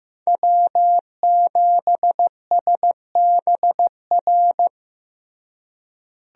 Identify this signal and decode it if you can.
Morse code, heard as a keyed tone, about 15 wpm, 700 Hz. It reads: W7SBR
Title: USA Call Signs - 5 Characters - Repeated 3x - 15wpm